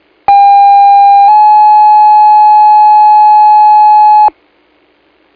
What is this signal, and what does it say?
Two-tone page, heard as a one-second tone, then a three-second tone, about 800 then 850 Hz. It is heard from 0.3 to 4.3 s.